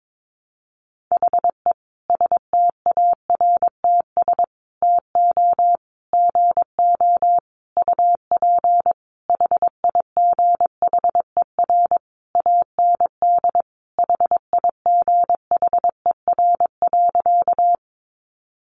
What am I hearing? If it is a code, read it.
Morse code, heard as a keyed tone, about 22 wpm, 700 Hz. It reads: HE STARTS TO GO UP HIGHER AND HIGHER.
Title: Additional Easy Phrases 22wpm